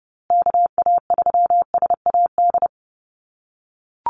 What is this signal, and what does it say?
Morse code, heard as a keyed tone, about 30 wpm, 700 Hz. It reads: KA3SAD
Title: USA Call Signs Repeated 2x 30wpm